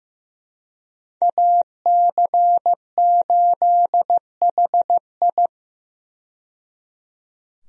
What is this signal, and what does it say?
Morse code, heard as a keyed tone, about 15 wpm, 700 Hz. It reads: AC8HI
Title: USA Call Signs - 5 Characters - 15wpm